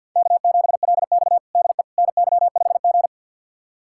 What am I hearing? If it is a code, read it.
Morse code, heard as a keyed tone, about 51 wpm, 700 Hz. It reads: K7LX DE NY5G
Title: USA Callsign DE Callsign - 4 Characters - 50wpm